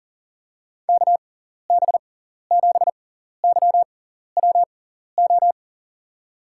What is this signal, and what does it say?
Morse code, heard as a keyed tone, about 40 wpm, spaced as for 14 wpm, 700 Hz. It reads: KB7YWO
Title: USA Call Signs - 6 Characters - 6x Character Spacing - 40wpm